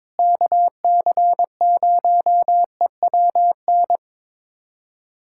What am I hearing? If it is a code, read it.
Morse code, heard as a keyed tone, about 22 wpm, 700 Hz. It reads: KC0EWN
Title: USA Callsign DE Callsign 22wpm